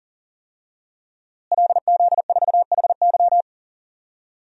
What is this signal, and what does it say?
Morse code, heard as a keyed tone, about 37 wpm, 700 Hz. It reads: LZ4HY
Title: DXCC-100 Call Signs - 5 Characters - Repeated 3x - 40wpm